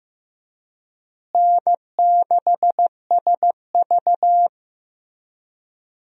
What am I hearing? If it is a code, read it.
Morse code, heard as a keyed tone, about 15 wpm, 700 Hz. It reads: N6SV
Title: USA Call Signs - 4 Characters - Repeated 3x - 15wpm